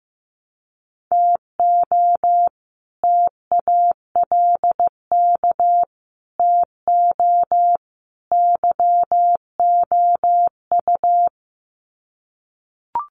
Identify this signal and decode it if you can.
Morse code, heard as a keyed tone, about 15 wpm, 700 Hz. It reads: TO TALK TO YOU
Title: Sets of 4 Words 15wpm